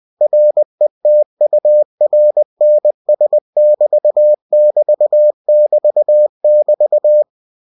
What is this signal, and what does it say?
Morse code, heard as a keyed tone, about 20 wpm, 600 Hz. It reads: RETURNS<BT><BT><BT><BT>
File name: cw-books / winnie-the-pooh / winnie-20wpm-page-0047